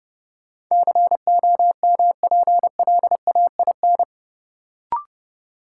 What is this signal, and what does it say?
Morse code, heard as a keyed tone, about 30 wpm, 700 Hz. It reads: COMPLAIN
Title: Sentences from Top 1300 Words - New Word Drill Alt - 30wpm